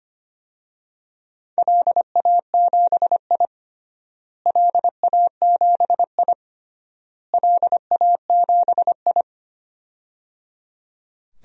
Morse code, heard as a keyed tone, about 25 wpm, 700 Hz. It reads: LA7I LA7I LA7I
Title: DXCC-100 Call Signs - 4 Characters - Repeated 3x - 25wpm